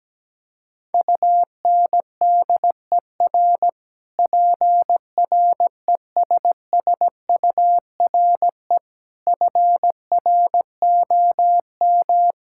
Morse code, heard as a keyed tone, about 17 wpm, 700 Hz. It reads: UNDER PRESSURE FROM
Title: Sentences from Top 1000 Words - Review Part 6 - 17wpm